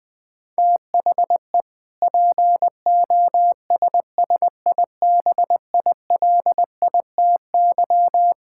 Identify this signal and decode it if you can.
Morse code, heard as a keyed tone, about 20 wpm, 700 Hz. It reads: THE POSSIBILITY
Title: Sentences with 3-Word Phrases - 20wpm